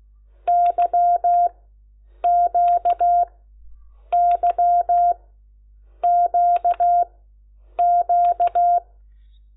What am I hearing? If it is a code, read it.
Morse code, heard as a keyed tone, about 16 wpm, spaced as for 8 wpm, 700 Hz. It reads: YQYQQ
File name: Lecciones